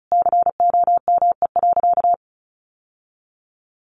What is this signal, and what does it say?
Morse code, heard as a keyed tone, about 34 wpm, 700 Hz. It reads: COME.
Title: Sentences from Top 1500 Words 35wpm